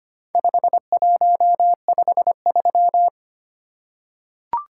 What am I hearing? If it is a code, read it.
Morse code, heard as a keyed tone, about 25 wpm, 700 Hz. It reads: 5153
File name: practice / 25wpm / Four Letter-Number Combinations - 25wpm